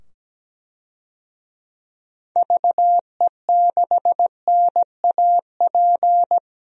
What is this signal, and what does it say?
Morse code, heard as a keyed tone, about 17 wpm, 700 Hz. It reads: VE6NAP
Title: DXCC-100 Call Signs - 6 Characters - 17wpm